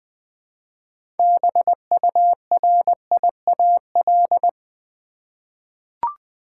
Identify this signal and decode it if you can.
Morse code, heard as a keyed tone, about 20 wpm, 700 Hz. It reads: BURIAL
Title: 6 Letter Words - First 3 Letters Given - 20wpm